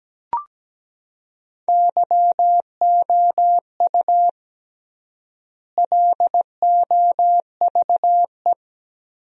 Morse code, heard as a keyed tone, about 17 wpm, 700 Hz. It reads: YOU LOVE
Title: 3x Word-Spacing - Sentences from Top 500 Words - 17wpm